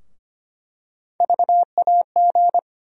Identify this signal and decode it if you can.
Morse code, heard as a keyed tone, about 25 wpm, 700 Hz. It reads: VAG
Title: USA Call Sign Postfixes - 3 Letters 25wpm